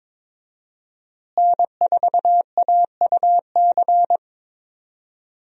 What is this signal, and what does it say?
Morse code, heard as a keyed tone, about 22 wpm, 700 Hz. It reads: N4AUC
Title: USA Call Signs - 5 Characters - Repeated 2x - 22wpm